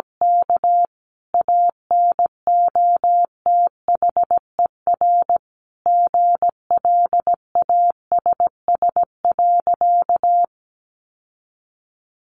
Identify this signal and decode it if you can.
Morse code, heard as a keyed tone, about 17 wpm, 700 Hz. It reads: K ANOTHER GLASS.
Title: Sentences from Top 1500 Words 17wpm